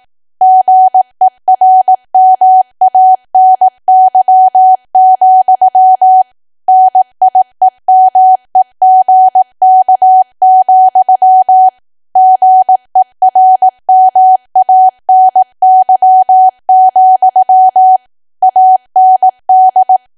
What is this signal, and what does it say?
Morse code, heard as a keyed tone, about 18 wpm, 750 Hz. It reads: GERMANY, NIEMEGK, GERMANY, AND
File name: cwa / W1AW-Level2 / 18WPM